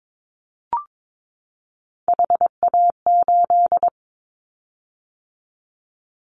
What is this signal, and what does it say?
Morse code, heard as a keyed tone, about 22 wpm, 700 Hz. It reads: HA8
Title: Callsign Prefixes for Top 25 DXCC - 22wpm